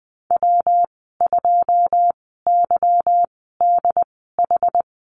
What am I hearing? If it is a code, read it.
Morse code, heard as a keyed tone, about 20 wpm, spaced as for 15 wpm, 700 Hz. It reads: W2YDH
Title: 161 Lesson 26g - Letter g - Call Signs 20wpm